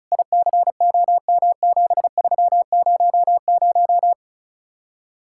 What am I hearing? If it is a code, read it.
Morse code, heard as a keyed tone, about 43 wpm, 700 Hz. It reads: ICOM7300
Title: Speed-Racing Common QSO Phrases Encore - 35wpm